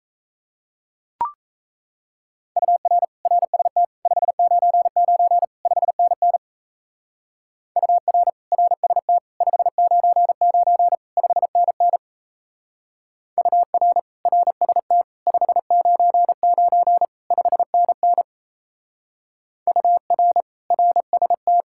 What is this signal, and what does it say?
Morse code, heard as a keyed tone, about 33 wpm, 700 Hz. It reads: UR RST 599 5NN UR RST 599 5NN UR RST 599 5NN UR RST
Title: Speed-Racing Common QSO Phrases - 28wpm